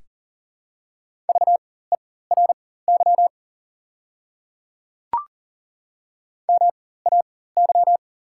Morse code, heard as a keyed tone, about 40 wpm, spaced as for 19 wpm, 690 Hz. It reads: VERY MAY
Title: 4x Character Spacing - Top 200 Words - 40wpm